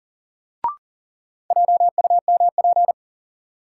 Morse code, heard as a keyed tone, about 40 wpm, 700 Hz. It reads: JUMP
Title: Sentences from Top 1300 Words - New Word Drill - 40wpm